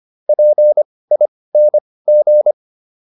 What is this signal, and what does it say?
Morse code, heard as a keyed tone, about 25 wpm, spaced as for 18 wpm, 600 Hz. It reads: PING